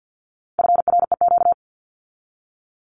Morse code, heard as a keyed tone, about 48 wpm, 700 Hz. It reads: FREQ
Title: Abbreviations 50wpm